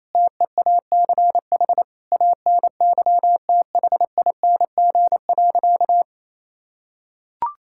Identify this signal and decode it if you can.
Morse code, heard as a keyed tone, about 27 wpm, 700 Hz. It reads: TEACH ANYTHING.
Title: Sentences from Top 1000 Words - Review Part 6 - 28wpm